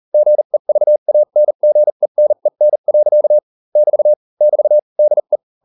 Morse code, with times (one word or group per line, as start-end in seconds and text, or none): 0.1-3.4 GEVANGENEN.
3.8-4.1 <BT>
4.4-4.8 <BT>
5.0-5.4 DE